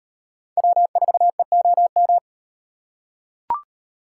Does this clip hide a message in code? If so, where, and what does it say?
Yes: Morse code from 0.6 to 2.2 s, heard as a keyed tone, about 35 wpm, 700 Hz. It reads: W4EOM